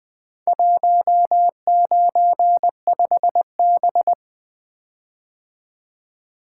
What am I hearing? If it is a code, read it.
Morse code, heard as a keyed tone, about 20 wpm, 700 Hz. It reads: 195B
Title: Four Letter-Number Combinations - 20wpm